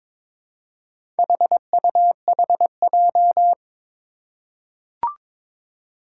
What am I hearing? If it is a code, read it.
Morse code, heard as a keyed tone, about 22 wpm, 700 Hz. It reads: HUHJ
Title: Four Challenging Letter-Number Combinations 22wpm